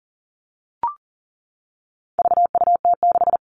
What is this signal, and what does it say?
Morse code, heard as a keyed tone, about 40 wpm, 700 Hz. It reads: VUT6
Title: All Cut Numbers - 4-Digit 40wpm